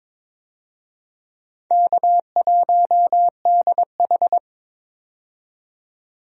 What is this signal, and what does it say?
Morse code, heard as a keyed tone, about 22 wpm, 700 Hz. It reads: K1DH
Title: USA Call Signs - 4 Characters - 22wpm